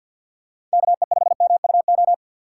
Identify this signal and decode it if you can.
Morse code, heard as a keyed tone, about 51 wpm, 700 Hz. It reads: KE5MUO
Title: USA Call Signs - 6 Characters - 4x Character Spacing - 50wpm